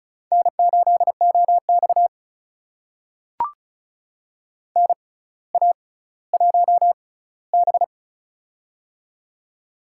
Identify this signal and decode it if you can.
Morse code, heard as a keyed tone, about 36 wpm, spaced as for 18 wpm, 700 Hz. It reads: N8OX NA1B
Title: USA Call Signs - 4 Characters - 6x Character Spacing - 35wpm